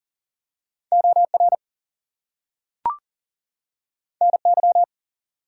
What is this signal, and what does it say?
Morse code, heard as a keyed tone, about 40 wpm, 700 Hz. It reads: OR NY
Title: US States and Canadian Provinces 40wpm